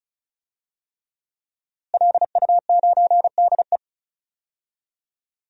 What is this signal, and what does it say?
Morse code, heard as a keyed tone, about 36 wpm, 690 Hz. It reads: LU9DE
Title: DXCC-100 Call Signs - 5 Characters - Repeated 3x - 35wpm